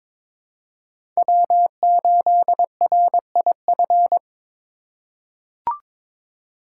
Morse code, heard as a keyed tone, about 21 wpm, 700 Hz. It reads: W8RIF
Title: USA Call Signs - 5 Characters - 6x Character Spacing - 22wpm